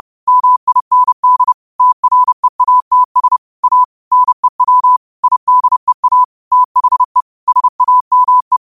Morse code, heard as a keyed tone, about 30 wpm, 1000 Hz. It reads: MIND TREATS A NEW IDEA THE SAME
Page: 30 WPM Morse code quotes